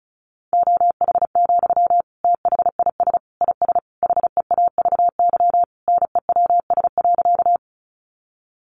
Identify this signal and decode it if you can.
Morse code, heard as a keyed tone, about 34 wpm, 700 Hz. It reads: OH, THIS IS HEAVY NEWS.
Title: Sentences from Top 1100 Words 35wpm